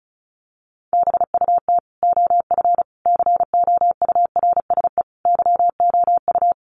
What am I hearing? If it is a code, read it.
Morse code, heard as a keyed tone, about 34 wpm, 700 Hz. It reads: BUT OF COURSE YOU
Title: Sentences from Top 1500 Words 35wpm